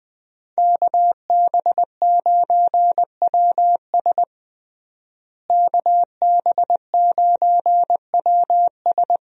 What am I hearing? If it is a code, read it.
Morse code, heard as a keyed tone, about 20 wpm, 700 Hz. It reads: KB9WS KB9WS
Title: USA Call Signs - 5 Characters - Repeated 2x - 20wpm